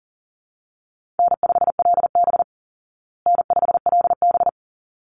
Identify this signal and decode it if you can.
Morse code, heard as a keyed tone, about 37 wpm, 700 Hz. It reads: N5LB N5LB
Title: Call Signs Repeated 3x - 40wpm